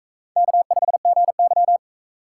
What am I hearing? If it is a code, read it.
Morse code, heard as a keyed tone, about 41 wpm, 700 Hz. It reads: K5GY